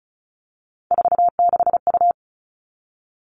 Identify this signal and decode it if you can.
Morse code, heard as a keyed tone, about 34 wpm, 700 Hz. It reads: 46U
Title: All Cut Numbers - 3-Digit 35wpm